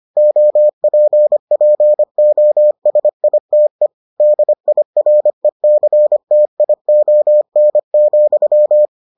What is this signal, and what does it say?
Morse code, heard as a keyed tone, about 25 wpm, 600 Hz. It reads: OPPOSITE DIRECTION,